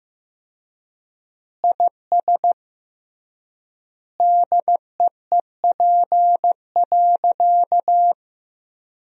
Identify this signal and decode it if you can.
Morse code, heard as a keyed tone, about 15 wpm, 700 Hz. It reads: IS DEEP.
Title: 3x Word-Spacing - Sentences from Top 500 Words - 15wpm